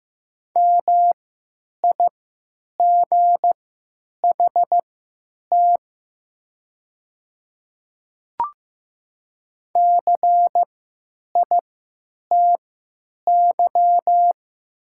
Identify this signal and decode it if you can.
Morse code, heard as a keyed tone, about 15 wpm, spaced as for 9 wpm, 700 Hz. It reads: MIGHT CITY
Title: 3x Character Spacing - Top 300 Words - 15wpm